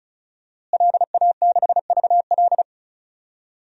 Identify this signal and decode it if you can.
Morse code, heard as a keyed tone, about 34 wpm, 700 Hz. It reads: LA6VL
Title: DXCC-100 Call Signs - 5 Characters - Repeated 3x - 35wpm